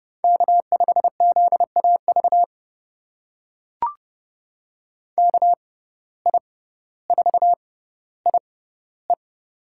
Morse code, heard as a keyed tone, about 30 wpm, spaced as for 15 wpm, 700 Hz. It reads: K5ZAV KI4IE
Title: USA Call Signs - 5 Characters - 6x Character Spacing - 30wpm